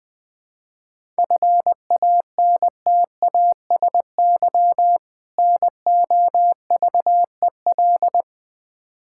Morse code, heard as a keyed tone, about 20 wpm, 700 Hz. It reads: FANTASY NOVEL
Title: Difficult Words with Hints - 20wpm